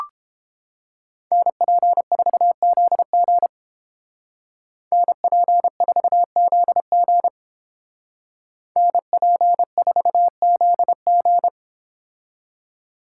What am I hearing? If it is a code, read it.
Morse code, heard as a keyed tone, about 30 wpm, 700 Hz. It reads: NP4ZG NP4ZG NP4ZG